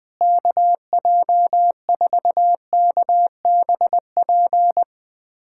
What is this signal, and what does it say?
Morse code, heard as a keyed tone, about 21 wpm, 700 Hz. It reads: KJ4KBP
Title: USA Callsign DE Callsign 20wpm